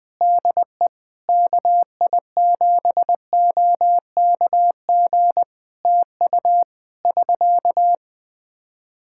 Morse code, heard as a keyed tone, about 20 wpm, 700 Hz. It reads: DE KI7OKG TU <SK>
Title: K1USN SST QSO Practice 20wpm